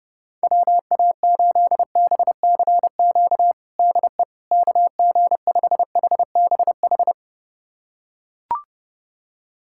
Morse code, heard as a keyed tone, about 32 wpm, 700 Hz. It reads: WA8BCQ DE KG5HBH
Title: USA Callsign DE Callsign - 6 Characters - 30wpm